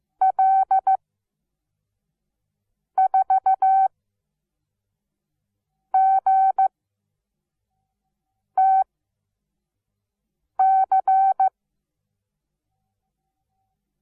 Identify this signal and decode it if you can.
Morse code, heard as a keyed tone, about 15 wpm, spaced as for 4 wpm, 770 Hz. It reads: L4GTC
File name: Onlinelehrgang / morsen / DJ4UF Morsen17